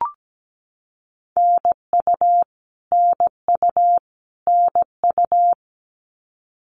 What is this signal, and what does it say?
Morse code, heard as a keyed tone, about 17 wpm, 700 Hz. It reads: NU NU NU
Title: Canadian Provinces and Territories - Repeated 3x - 17wpm